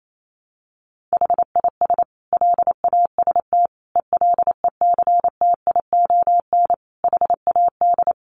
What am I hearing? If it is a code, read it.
Morse code, heard as a keyed tone, about 28 wpm, 700 Hz. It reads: HIS LAST ELECTION HAD
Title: Sentences from Top 1500 Words 28wpm